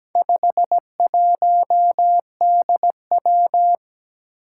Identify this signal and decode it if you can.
Morse code, heard as a keyed tone, about 17 wpm, 700 Hz. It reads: 51DW
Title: Four Challenging Letter-Number Combinations 17wpm